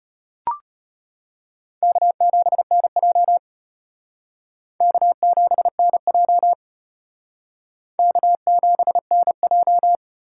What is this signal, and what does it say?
Morse code, heard as a keyed tone, about 32 wpm, 700 Hz. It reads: K7NJ K7NJ K7NJ